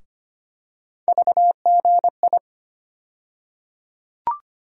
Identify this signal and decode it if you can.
Morse code, heard as a keyed tone, about 25 wpm, 700 Hz. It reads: VGI
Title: USA Call Sign Postfixes - 3 Letters 25wpm